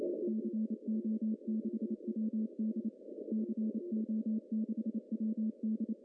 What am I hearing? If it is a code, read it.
Morse code, heard as a keyed tone, about 28 wpm, 220 Hz. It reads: CO6WD CO6WD
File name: CW / cqww-cw-k5tr